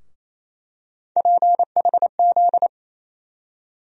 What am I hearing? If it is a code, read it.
Morse code, heard as a keyed tone, about 28 wpm, 700 Hz. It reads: PHZ
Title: USA Call Sign Postfixes - 3 Letters 28wpm